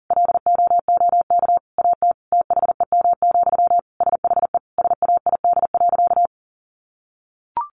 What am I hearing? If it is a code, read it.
Morse code, heard as a keyed tone, about 40 wpm, 700 Hz. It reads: LOOK AT THEM, SHE SAID.
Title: Sentences from Top 1400 Words 40wpm